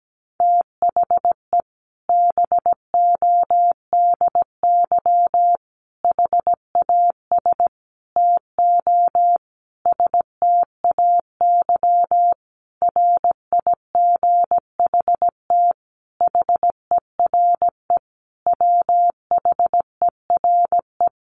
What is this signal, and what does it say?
Morse code, heard as a keyed tone, about 17 wpm, 700 Hz. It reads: THE BODY HAS TO STAY RIGHT HERE WHERE
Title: A Sound Of Thunder 17wpm